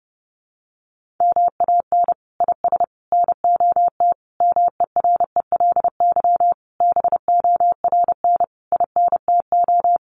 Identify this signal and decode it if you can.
Morse code, heard as a keyed tone, about 30 wpm, 700 Hz. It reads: MAN IS NOT MERELY BORN INTO
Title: Sentences from Top 1400 Words 30wpm